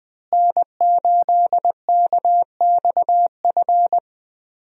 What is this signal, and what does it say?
Morse code, heard as a keyed tone, about 20 wpm, 700 Hz. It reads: N8KXF
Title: USA Call Signs - 5 Characters - Repeated 3x - 20wpm